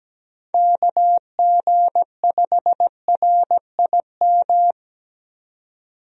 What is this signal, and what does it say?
Morse code, heard as a keyed tone, about 17 wpm, 700 Hz. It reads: KG5RIM
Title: USA Call Signs - 6 Characters - 17wpm